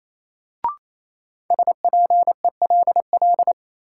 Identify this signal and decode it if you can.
Morse code, heard as a keyed tone, about 28 wpm, 700 Hz. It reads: SPELL
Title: Sentences from Top 1000 Words - Review Part 1 - New Word Drill - 28wpm